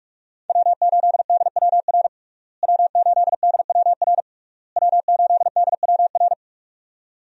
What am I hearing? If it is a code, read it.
Morse code, heard as a keyed tone, about 44 wpm, 700 Hz. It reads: W8DWR W8DWR W8DWR
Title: USA Call Signs - 5 Characters - Repeated 3x - 45wpm